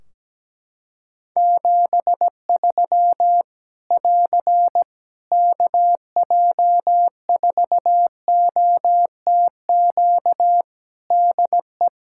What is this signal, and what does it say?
Morse code, heard as a keyed tone, about 17 wpm, 700 Hz. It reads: 73 <AR> KJ4OTQ DE
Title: Practice QSO - Exchange 4 - Ending - 17wpm